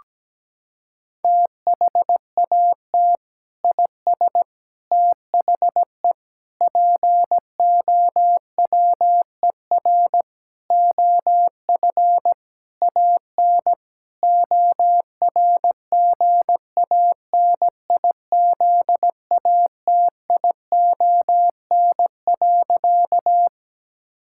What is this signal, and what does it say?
Morse code, heard as a keyed tone, about 17 wpm, 700 Hz. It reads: THAT IS THE POWER OF AN ORGANIZATION.
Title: Sentences from Top 1000 Words - Review Part 5 - 17wpm